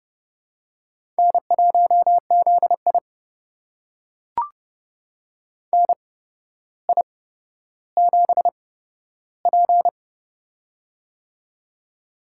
Morse code, heard as a keyed tone, about 31 wpm, spaced as for 13 wpm, 700 Hz. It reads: N1ZI NI7P
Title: USA Call Signs - 4 Characters - 8x Character Spacing - 30wpm